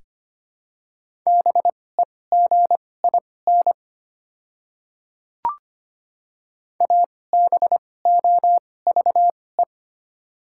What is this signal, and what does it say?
Morse code, heard as a keyed tone, about 25 wpm, spaced as for 18 wpm, 690 Hz. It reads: BEGIN ABOVE
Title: 2x Character Spacing - Top 500 Words - 25wpm